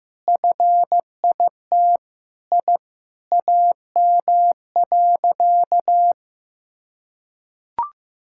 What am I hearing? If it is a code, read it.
Morse code, heard as a keyed tone, about 16 wpm, 700 Hz. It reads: FIT I AM.
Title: Sentences from Top 1000 Words - Review Part 5 - 15wpm